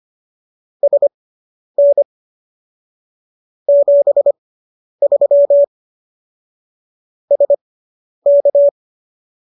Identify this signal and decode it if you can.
Morse code, heard as a keyed tone, about 25 wpm, spaced as for 10 wpm, 570 Hz. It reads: SN 73 SK